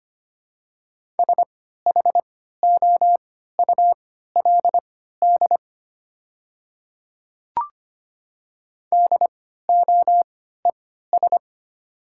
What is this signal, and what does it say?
Morse code, heard as a keyed tone, about 25 wpm, spaced as for 14 wpm, 700 Hz. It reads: SHOULD DOES
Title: 3x Character Spacing - Top 300 Words - 25wpm